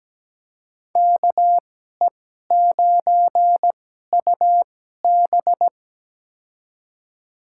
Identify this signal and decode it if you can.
Morse code, heard as a keyed tone, about 17 wpm, spaced as for 12 wpm, 700 Hz. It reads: KE9UB
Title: USA Call Signs - 5 Characters - 2x Character Spacing - 17wpm